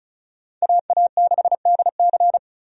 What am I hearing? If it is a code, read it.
Morse code, heard as a keyed tone, about 34 wpm, 700 Hz. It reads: AA6DC
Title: USA Call Signs - 5 Characters - 35wpm